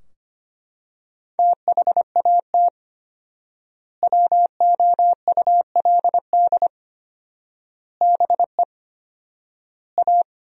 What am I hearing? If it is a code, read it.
Morse code, heard as a keyed tone, about 25 wpm, 700 Hz. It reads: THAT WOULD BE A
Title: 4x Word-Spacing - Sentences from Top 500 Words - 25wpm